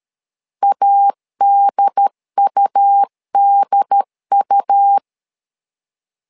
Morse code, heard as a keyed tone, about 13 wpm, 800 Hz. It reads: ADUDU
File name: LEZIONI / 2 UHDN / 2 UHDN 12wpm 1456